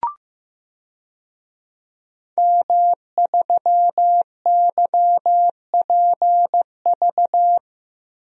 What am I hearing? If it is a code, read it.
Morse code, heard as a keyed tone, about 15 wpm, 700 Hz. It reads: M3YPV
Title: DXCC-100 Call Signs - 5 Characters - Repeated 2x - 15wpm